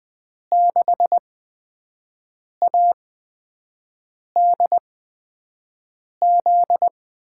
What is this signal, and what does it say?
Morse code, heard as a keyed tone, about 20 wpm, spaced as for 5 wpm, 700 Hz. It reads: 6ADZ
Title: USA Call Signs - 5 Characters - 8x Character Spacing - 20wpm